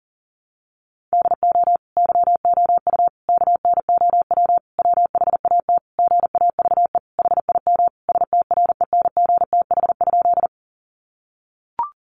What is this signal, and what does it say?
Morse code, heard as a keyed tone, about 37 wpm, 700 Hz. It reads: DO YOU KNOW WHAT GAVE HIM STRENGTH?
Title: Sentences from Top 1300 Words 40wpm